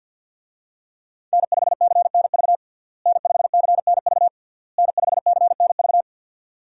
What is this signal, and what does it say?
Morse code, heard as a keyed tone, about 47 wpm, 700 Hz. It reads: N5CNV N5CNV N5CNV
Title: USA Call Signs - 5 Characters - Repeated 3x - 50wpm